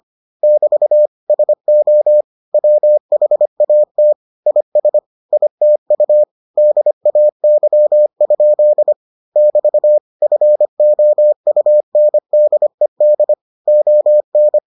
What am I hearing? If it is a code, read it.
Morse code, heard as a keyed tone, about 25 wpm, 600 Hz. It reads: <BT> SO WHAT IS ITU DAY? <BT> FOUNDED ON